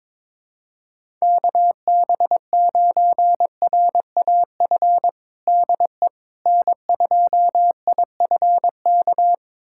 Morse code, heard as a keyed tone, about 22 wpm, 700 Hz. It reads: KB9RAF DE N2IFK
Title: USA Callsign DE Callsign 22wpm